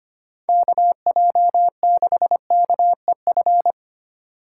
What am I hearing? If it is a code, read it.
Morse code, heard as a keyed tone, about 25 wpm, 700 Hz. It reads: KJ6KEF
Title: USA Call Signs - 6 Characters - 2x Character Spacing - 25wpm